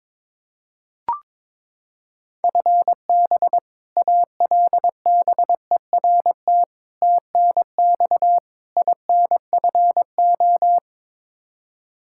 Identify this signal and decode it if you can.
Morse code, heard as a keyed tone, about 22 wpm, 700 Hz. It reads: FB ALBERT TNX INFO
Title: Practice QSO - Exchange 3 - More Info - 22wpm